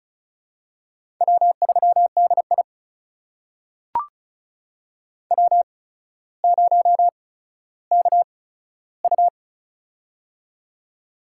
Morse code, heard as a keyed tone, about 36 wpm, spaced as for 15 wpm, 700 Hz. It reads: W3DI W0KU
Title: USA Call Signs - 4 Characters - 8x Character Spacing - 35wpm